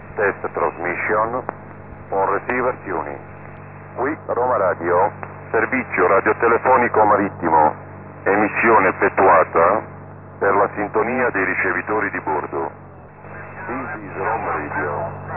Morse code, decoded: none